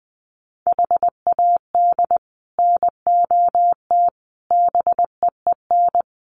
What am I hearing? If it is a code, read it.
Morse code, heard as a keyed tone, about 20 wpm, 700 Hz. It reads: HAD NOT BEEN
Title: Sets of 5 Words Encore 20wpm